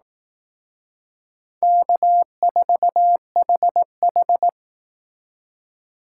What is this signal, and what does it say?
Morse code, heard as a keyed tone, about 18 wpm, 700 Hz. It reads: K4HH
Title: Speed Racing - USA Call Signs 15wpm